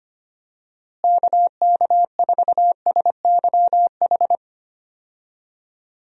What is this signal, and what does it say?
Morse code, heard as a keyed tone, about 25 wpm, 700 Hz. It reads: KK4SYH
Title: USA Call Signs - 6 Characters - Repeated 3x - 25wpm